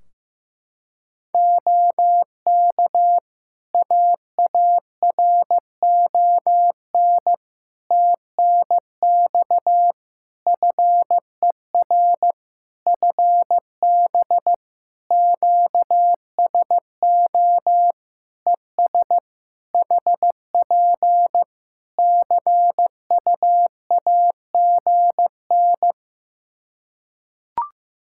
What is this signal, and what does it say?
Morse code, heard as a keyed tone, about 15 wpm, 700 Hz. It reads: OK AARON TNX FER FB QSO ES HP CUAGN
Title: K1USN SST QSO Practice 15wpm